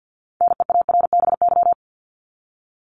Morse code, heard as a keyed tone, about 50 wpm, 700 Hz. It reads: NEARBY
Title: Top 1501-1750 Words 50wpm